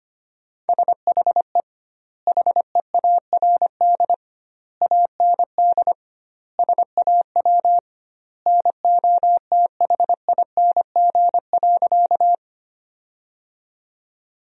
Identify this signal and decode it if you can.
Morse code, heard as a keyed tone, about 25 wpm, 700 Hz. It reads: SHE HEARD AND SAW NOTHING.
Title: 2x Word-Spacing - Sentences from Top 500 Words - 25wpm